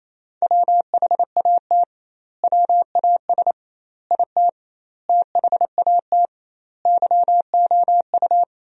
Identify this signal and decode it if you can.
Morse code, heard as a keyed tone, about 28 wpm, 700 Hz. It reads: WHAT WAS IT THAT YOU
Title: 2x Word-Spacing - Sentences from Top 300 Words - 28wpm